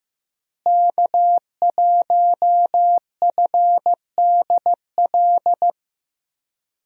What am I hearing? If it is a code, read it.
Morse code, heard as a keyed tone, about 15 wpm, 700 Hz. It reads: K1FDL
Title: USA Call Signs - 5 Characters - Repeated 2x - 15wpm